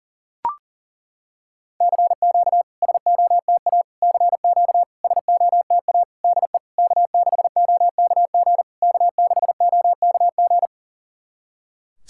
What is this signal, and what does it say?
Morse code, heard as a keyed tone, about 39 wpm, 700 Hz. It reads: CQ SOTA CQ SOTA DE K6OKG K6OKG
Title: SOTA QSO Practice 40wpm